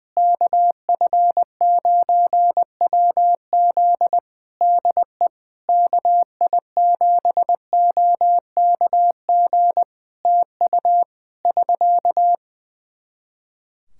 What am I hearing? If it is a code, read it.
Morse code, heard as a keyed tone, about 20 wpm, 700 Hz. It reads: KF9WZ DE KI7OKG TU <SK>
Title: K1USN SST QSO Practice 20wpm